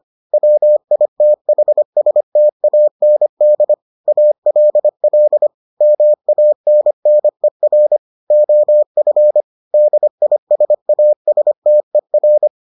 Morse code, heard as a keyed tone, about 25 wpm, 600 Hz. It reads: WITHSTAND ALL MANNER OF DISASTER